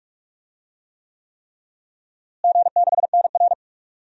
Morse code, heard as a keyed tone, about 46 wpm, 700 Hz. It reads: G6NR
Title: DXCC-100 Call Signs - 4 Characters - Repeated 3x - 45wpm